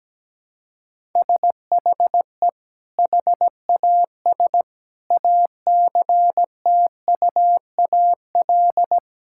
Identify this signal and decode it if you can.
Morse code, heard as a keyed tone, about 17 wpm, 700 Hz. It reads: SHE HAS ACTUAL
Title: Sentences from Top 1000 Words - Review Part 5 - 17wpm